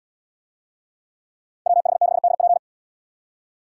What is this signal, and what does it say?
Morse code, heard as a keyed tone, about 77 wpm, 700 Hz. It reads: WH6DP